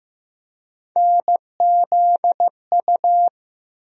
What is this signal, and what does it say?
Morse code, heard as a keyed tone, about 15 wpm, 700 Hz. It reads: NZU
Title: USA Call Sign Postfixes - 3 Letters 15wpm